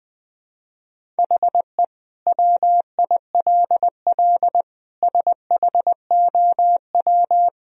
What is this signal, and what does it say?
Morse code, heard as a keyed tone, about 20 wpm, 700 Hz. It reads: HE WILL SHOW
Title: Sentences from Top 1000 Words - Review Part 1 - 20wpm